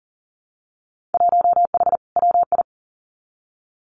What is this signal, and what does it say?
Morse code, heard as a keyed tone, about 40 wpm, 700 Hz. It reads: 1H WI
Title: Winter Field Day Exchanges 40wpm